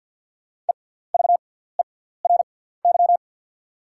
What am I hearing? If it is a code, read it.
Morse code, heard as a keyed tone, about 51 wpm, spaced as for 17 wpm, 700 Hz. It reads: EVERY
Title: 6x Character Spacing - Top 300 Words - 50wpm